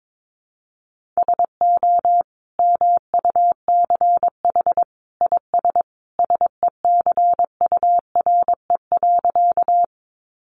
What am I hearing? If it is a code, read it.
Morse code, heard as a keyed tone, about 22 wpm, 700 Hz. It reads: SO MUCH IS SECURE.
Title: Sentences from Top 1200 Words 22wpm